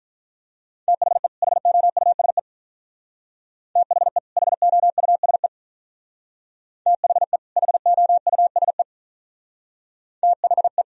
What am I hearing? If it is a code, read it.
Morse code, heard as a keyed tone, about 44 wpm, 700 Hz. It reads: THE HOUSE THE HOUSE THE HOUSE THE